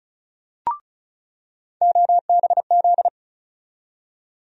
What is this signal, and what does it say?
Morse code, heard as a keyed tone, about 36 wpm, 700 Hz. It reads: OBZ
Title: USA Call Sign Postfixes - 3 Letters 35wpm